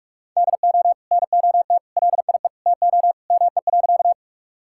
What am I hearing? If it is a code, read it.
Morse code, heard as a keyed tone, about 46 wpm, 700 Hz. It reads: DO NOT LIE TO ME.
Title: Sentences from Top 1600 Words 45wpm